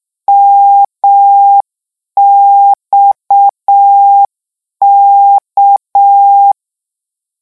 Morse code, heard as a keyed tone, about 6 wpm, 800 Hz. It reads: MXK
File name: lecon 1-vitesse 6